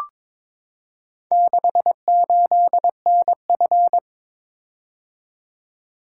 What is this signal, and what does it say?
Morse code, heard as a keyed tone, about 22 wpm, 700 Hz. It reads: 68NF
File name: practice / 22wpm / Four Letter-Number Combinations - 22wpm